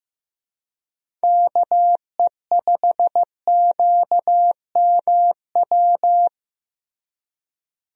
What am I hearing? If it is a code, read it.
Morse code, heard as a keyed tone, about 15 wpm, 700 Hz. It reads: KE5QMW
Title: USA Call Signs - 6 Characters - Repeated 3x - 15wpm